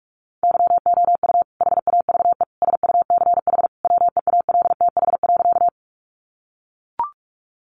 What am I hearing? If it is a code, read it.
Morse code, heard as a keyed tone, about 44 wpm, 700 Hz. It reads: YOU HAVE SUCH WEALTH.
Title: Sentences from Top 1200 Words 45wpm